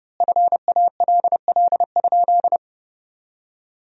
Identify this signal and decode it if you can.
Morse code, heard as a keyed tone, about 30 wpm, 700 Hz. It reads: FALL?
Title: 3x Word-Spacing - Sentences from Top 500 Words - 30wpm